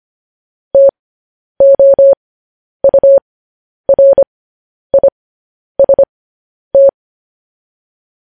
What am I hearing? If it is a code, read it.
Morse code, heard as a keyed tone, about 25 wpm, spaced as for 10 wpm, 550 Hz. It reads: TOURIST
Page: news-headlines-in-morse-code-at-10-wpm